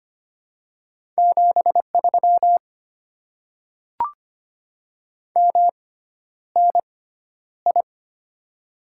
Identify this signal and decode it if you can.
Morse code, heard as a keyed tone, about 25 wpm, spaced as for 11 wpm, 700 Hz. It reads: 73 MNI
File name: Top 90 QSO Elements - 6x Character Spacing - 25wpm